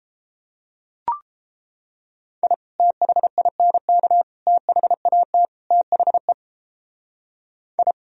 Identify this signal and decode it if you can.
Morse code, heard as a keyed tone, about 32 wpm, 700 Hz. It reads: I THINK THAT THE I